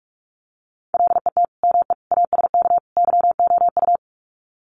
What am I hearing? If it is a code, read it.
Morse code, heard as a keyed tone, about 44 wpm, 700 Hz. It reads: LET ME ASK YOU
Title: Sets of 4 Words 45wpm